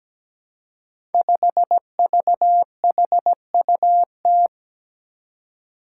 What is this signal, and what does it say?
Morse code, heard as a keyed tone, about 17 wpm, 700 Hz. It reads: 5VHUT
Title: Five Challenging Letter-Number Combinations 17wpm